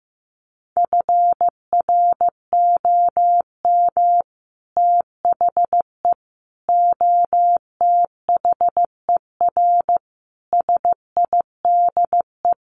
Morse code, heard as a keyed tone, about 15 wpm, 700 Hz. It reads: FROM THE OTHER SIDE
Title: Sets of 5 Words Encore 15wpm